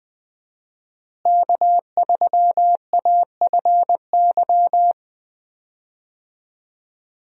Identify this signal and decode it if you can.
Morse code, heard as a keyed tone, about 20 wpm, 700 Hz. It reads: K3AFY
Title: USA Call Signs - 5 Characters - Repeated 2x - 20wpm